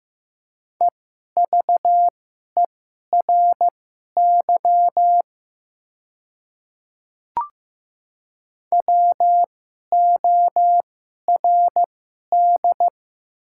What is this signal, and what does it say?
Morse code, heard as a keyed tone, about 15 wpm, spaced as for 11 wpm, 700 Hz. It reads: EVERY WORD
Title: 2x Character Spacing - Top 300 Words - 15wpm